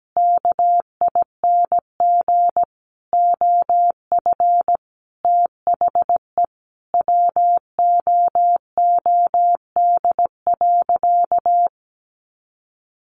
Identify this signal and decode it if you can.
Morse code, heard as a keyed tone, about 17 wpm, 700 Hz. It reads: KING OF THE WOOD.
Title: Sentences from Top 1200 Words 17wpm